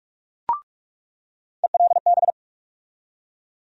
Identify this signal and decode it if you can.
Morse code, heard as a keyed tone, about 42 wpm, 700 Hz. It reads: ELB
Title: USA Call Sign Postfixes - 3 Letters 45wpm